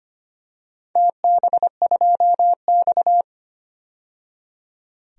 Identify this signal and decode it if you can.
Morse code, heard as a keyed tone, about 25 wpm, 700 Hz. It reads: TB2X
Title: DXCC-100 Call Signs - 4 Characters - 25wpm